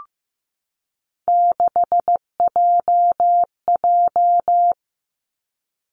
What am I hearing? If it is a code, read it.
Morse code, heard as a keyed tone, about 15 wpm, 700 Hz. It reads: 6JJ
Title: Three Challenging Letter-Number Combinations 15wpm